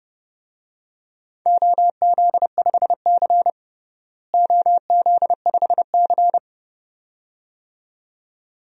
Morse code, heard as a keyed tone, about 32 wpm, 700 Hz. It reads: OZ5C OZ5C
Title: DXCC-100 Call Signs - 4 Characters - Repeated 2x - 30wpm